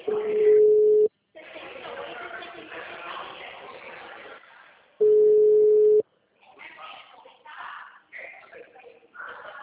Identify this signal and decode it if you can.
Ringback tone (Europe, 425 Hz), with speech.